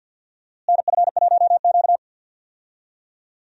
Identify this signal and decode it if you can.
Morse code, heard as a keyed tone, about 52 wpm, 700 Hz. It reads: NF1Q
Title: USA Call Signs - 4 Characters - Repeated 3x - 50wpm